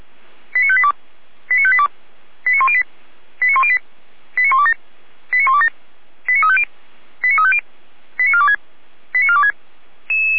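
Five-tone selective call (ZVEI), heard as quick runs of five tones, tones of about 70 ms. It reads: 89582 89582 89108 89108 89127 89127 89350 89350 89426 89426
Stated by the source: ZVEI five-tone selective call received on an SDR.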